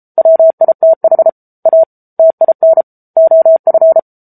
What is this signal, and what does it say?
Morse code, heard as a keyed tone, about 34 wpm, 650 Hz. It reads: WITH A TIN OF
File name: Book Two / 07-Chapter 7 - 33 WPM